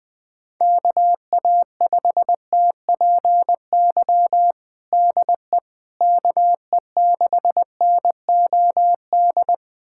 Morse code, heard as a keyed tone, about 20 wpm, 700 Hz. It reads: KA5TPY DE KE6NOD
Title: USA Callsign DE Callsign - 6 Characters - 20wpm